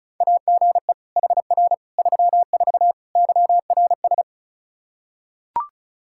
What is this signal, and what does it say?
Morse code, heard as a keyed tone, about 34 wpm, 700 Hz. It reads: AGE HR 34 YRS
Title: Practice QSO - Exchange 3 - More Info - 35wpm